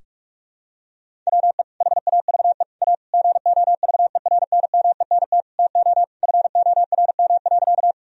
Morse code, heard as a keyed tone, about 46 wpm, 700 Hz. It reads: WE HAVE A GOVERNMENT TO FORM.
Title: Sentences from Top 1000 Words - Review Part 1 - 45wpm